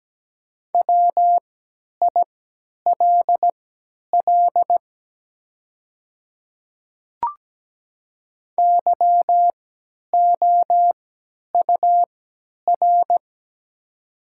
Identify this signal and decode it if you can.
Morse code, heard as a keyed tone, about 17 wpm, spaced as for 10 wpm, 700 Hz. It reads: WILL YOUR
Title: 3x Character Spacing - Top 200 Words - 17wpm